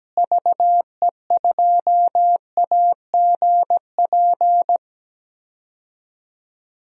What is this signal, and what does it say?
Morse code, heard as a keyed tone, about 17 wpm, 700 Hz. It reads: VE2AGP
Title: DXCC-100 Call Signs - 6 Characters - Repeated 2x - 17wpm